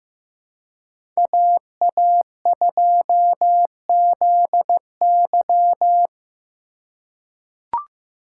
Morse code, heard as a keyed tone, about 15 wpm, 700 Hz. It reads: AA2ZY
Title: USA Call Signs - 5 Characters - 2x Character Spacing - 15wpm